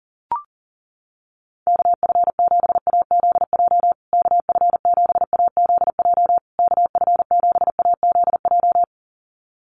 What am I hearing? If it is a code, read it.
Morse code, heard as a keyed tone, about 40 wpm, 700 Hz. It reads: KF7AZJ KF7AZJ KF7AZJ
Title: USA Call Signs Repeated 3x 40wpm